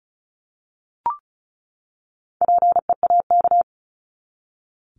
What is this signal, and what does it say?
Morse code, heard as a keyed tone, about 34 wpm, 700 Hz. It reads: PEAK
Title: Top 1751-2000 Words 35wpm